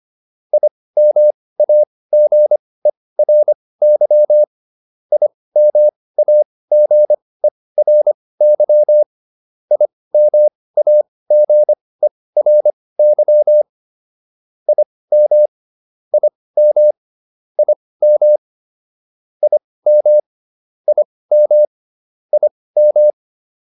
Morse code, heard as a keyed tone, about 25 wpm, spaced as for 18 wpm, 600 Hz. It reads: IMAGERY IMAGERY IMAGERY IM IM IM IM IM IM